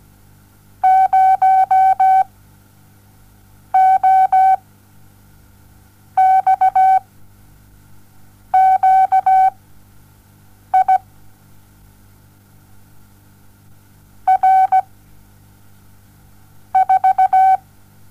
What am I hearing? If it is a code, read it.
Morse code, heard as a keyed tone, about 17 wpm, spaced as for 6 wpm, 760 Hz. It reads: 0 O X QI R 4
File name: Onlinelehrgang / morsen / DJ4UF Morsen07